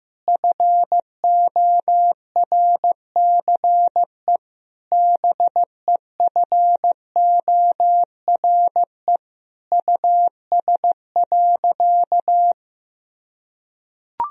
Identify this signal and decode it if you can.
Morse code, heard as a keyed tone, about 15 wpm, 700 Hz. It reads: FORCE BEFORE US.
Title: Sentences from Top 1000 Words - Review Part 3 - 15wpm